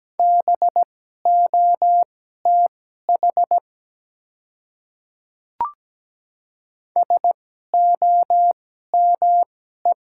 Morse code, heard as a keyed tone, about 17 wpm, spaced as for 12 wpm, 690 Hz. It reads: BOTH SOME
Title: 2x Character Spacing - Top 300 Words - 17wpm